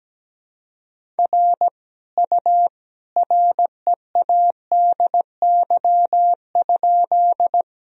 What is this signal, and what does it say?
Morse code, heard as a keyed tone, about 17 wpm, 700 Hz. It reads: R U READY?
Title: Sentences from Top 400 Words and Abbreviations 17wpm